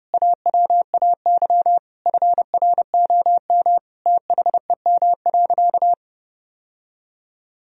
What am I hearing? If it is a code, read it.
Morse code, heard as a keyed tone, about 32 wpm, 700 Hz. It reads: AWAY FROM THEM.
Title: Sentences from Top 1000 Words - Review Part 3 - 30wpm